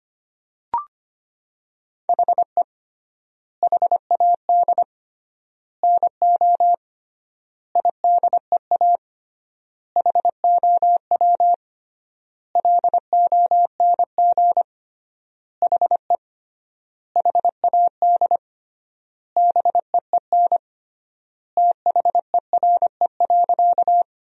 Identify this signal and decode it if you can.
Morse code, heard as a keyed tone, about 25 wpm, 700 Hz. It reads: HE HAD NO IDEA HOW LONG HE HAD BEEN THERE.
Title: 3x Word-Spacing - Sentences from Top 300 Words - 25wpm